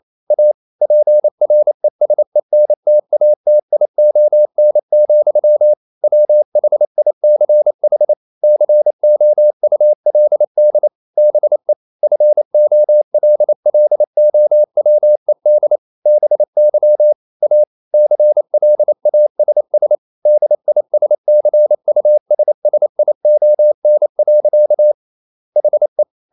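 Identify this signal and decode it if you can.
Morse code, heard as a keyed tone, about 28 wpm, 600 Hz. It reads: A PRESENTATION, WHICH COULD BE FOLLOWED BY A CLASS DISCUSSION. HE
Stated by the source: Morse code practice at 28 wpm, text of Ken Follett's novel Never in English, from CW med Gnister.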